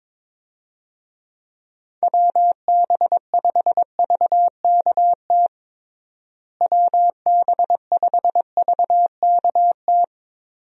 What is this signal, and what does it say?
Morse code, heard as a keyed tone, about 22 wpm, 700 Hz. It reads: WB5VKT WB5VKT
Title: USA Call Signs - 6 Characters - Repeated 3x - 22wpm